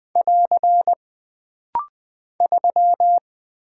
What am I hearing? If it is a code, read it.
Morse code, heard as a keyed tone, about 20 wpm, 700 Hz. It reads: <AR> 3
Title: Single Character (Letter:Number:Punctuation:Prosign) - Rapid-Fire - 20wpm